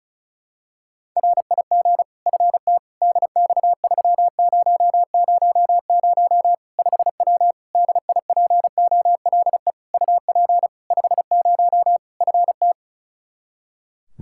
Morse code, heard as a keyed tone, about 34 wpm, 700 Hz. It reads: RIG FT DX3000 5W DIPOLE UP 50 FT
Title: Practice QSO - Exchange 2 - Info - 35wpm